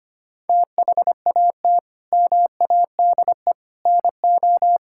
Morse code, heard as a keyed tone, about 25 wpm, 700 Hz. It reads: THAT MADE NO
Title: Sentences from Top 1000 Words - Review Part 5 - 25wpm